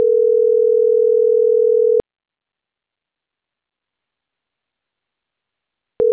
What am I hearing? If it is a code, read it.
Ringback tone (North America), 440 + 480 Hz.